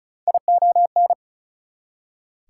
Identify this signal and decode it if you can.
Morse code, heard as a keyed tone, about 34 wpm, 700 Hz. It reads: ION
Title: Prefixes and Suffixes 35wpm